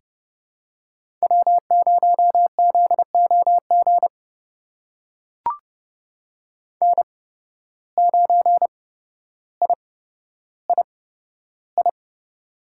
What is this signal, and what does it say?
Morse code, heard as a keyed tone, about 30 wpm, spaced as for 13 wpm, 700 Hz. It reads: W0ZOG N9III
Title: USA Call Signs - 5 Characters - 8x Character Spacing - 30wpm